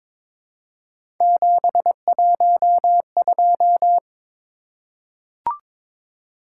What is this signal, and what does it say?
Morse code, heard as a keyed tone, about 22 wpm, 700 Hz. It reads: 712
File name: practice / 22wpm / Three Letter-Number Combinations - 22wpm